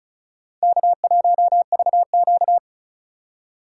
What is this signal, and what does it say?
Morse code, heard as a keyed tone, about 44 wpm, 700 Hz. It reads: K1VQ